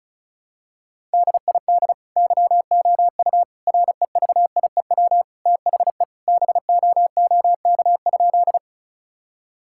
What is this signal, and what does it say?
Morse code, heard as a keyed tone, about 36 wpm, 700 Hz. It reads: DID YOU REVIEW THE BOOK?
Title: Sentences from Top 1000 Words - Review Part 6 - 35wpm